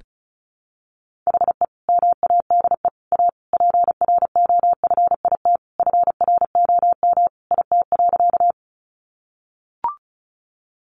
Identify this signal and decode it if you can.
Morse code, heard as a keyed tone, about 34 wpm, 700 Hz. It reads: HE MADE A PROFIT FROM IT.
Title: Sentences from Top 800 Words 35wpm